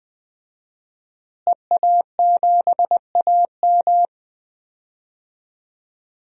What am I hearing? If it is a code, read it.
Morse code, heard as a keyed tone, about 20 wpm, 700 Hz. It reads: EA7AM
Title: DXCC-100 Call Signs - 5 Characters - Repeated 2x - 20wpm